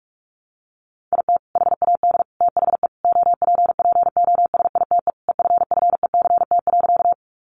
Morse code, heard as a keyed tone, about 44 wpm, 700 Hz. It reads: IT HAD THE OPPOSITE EFFECT.
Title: Sentences from Top 1300 Words 45wpm